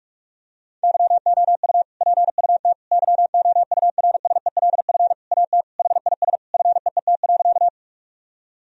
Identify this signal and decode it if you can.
Morse code, heard as a keyed tone, about 46 wpm, 700 Hz. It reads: YOU PUT YOURSELF AT HIS FEET.
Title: Sentences from Top 1000 Words - Review Part 1 - 45wpm